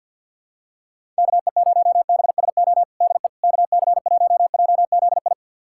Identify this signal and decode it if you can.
Morse code, heard as a keyed tone, about 48 wpm, 700 Hz. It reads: KE0BSO DE KC1JZI
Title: USA Callsign DE Callsign - 6 Characters - 50wpm